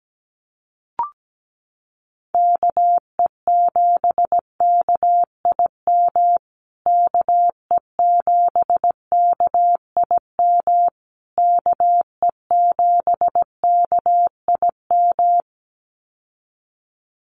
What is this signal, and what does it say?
Morse code, heard as a keyed tone, about 17 wpm, 700 Hz. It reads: KE7KIM KE7KIM KE7KIM
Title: USA Call Signs Repeated 3x 17wpm